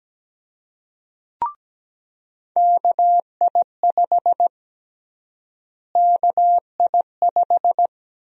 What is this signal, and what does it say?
Morse code, heard as a keyed tone, about 17 wpm, 700 Hz. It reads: KI5 KI5
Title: USA Call Sign Prefixes - Repeated 2x - 17wpm